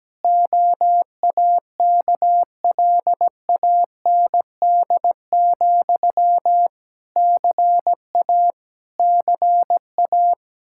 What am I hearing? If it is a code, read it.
Morse code, heard as a keyed tone, about 17 wpm, 700 Hz. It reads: OAKLAND, CA CA
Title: Practice QSO - Exchange 1 - Beginning - 17wpm